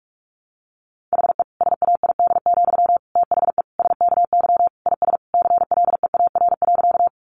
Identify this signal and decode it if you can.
Morse code, heard as a keyed tone, about 44 wpm, 700 Hz. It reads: HE SAID, THE SKY IS CLEAR.
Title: Sentences from Top 1200 Words 45wpm